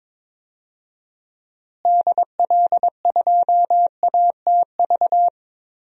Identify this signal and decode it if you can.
Morse code, heard as a keyed tone, about 22 wpm, 700 Hz. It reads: DL2ATV
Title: DXCC-100 Call Signs - 6 Characters - Repeated 3x - 22wpm